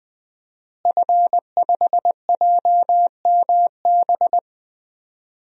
Morse code, heard as a keyed tone, about 20 wpm, 700 Hz. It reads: F5JMB
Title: DXCC-100 Call Signs - 5 Characters - Repeated 2x - 20wpm